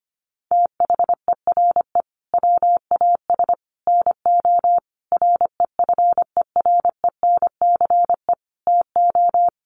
Morse code, heard as a keyed tone, about 25 wpm, 700 Hz. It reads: THERE WAS NO REFERENCE TO
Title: Sentences from Top 1200 Words 25wpm